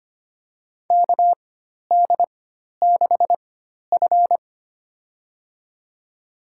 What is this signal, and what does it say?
Morse code, heard as a keyed tone, about 25 wpm, spaced as for 12 wpm, 700 Hz. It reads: KD6F
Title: USA Call Signs - 4 Characters - 4x Character Spacing - 25wpm